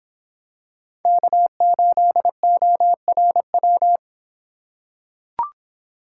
Morse code, heard as a keyed tone, about 26 wpm, 700 Hz. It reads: K8ORW